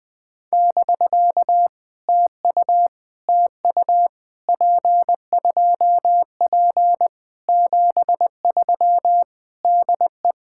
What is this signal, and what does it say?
Morse code, heard as a keyed tone, about 20 wpm, 700 Hz. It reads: <BK> TU TU P2P 73 DE
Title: POTA QSO Practice - 20wpm